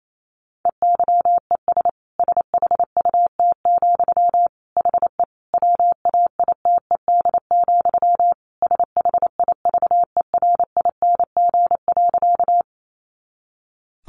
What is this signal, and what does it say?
Morse code, heard as a keyed tone, about 28 wpm, 700 Hz. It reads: EYES SHUT, HE WAITED, SHIVERING.
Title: A Sound Of Thunder 28wpm